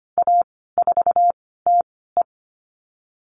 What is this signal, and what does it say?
Morse code, heard as a keyed tone, about 25 wpm, 700 Hz. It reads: A 4 T E
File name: Lesson 15g - Number 5 - Call Signs 25wpm